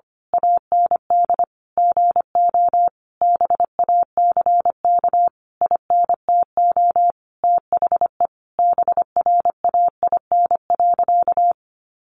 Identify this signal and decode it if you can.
Morse code, heard as a keyed tone, about 25 wpm, 700 Hz. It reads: AND GO BACK INTO THE BRAIN.
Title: A Sound Of Thunder 25wpm